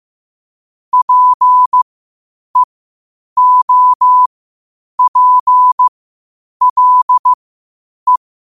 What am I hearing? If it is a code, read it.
Morse code, heard as a keyed tone, about 15 wpm, spaced as for 8 wpm, 1000 Hz. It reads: PEOPLE
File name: qotdfiles / 8wpm